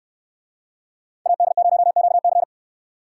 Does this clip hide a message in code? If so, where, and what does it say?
Yes: Morse code from 1.3 to 2.4 s, heard as a keyed tone, about 68 wpm, 700 Hz. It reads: AD0QC